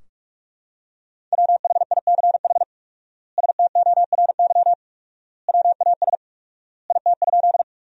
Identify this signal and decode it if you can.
Morse code, heard as a keyed tone, about 42 wpm, 700 Hz. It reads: WHICH STORY WAS IT?
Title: 4x Word-Spacing - Sentences from Top 300 Words - 45wpm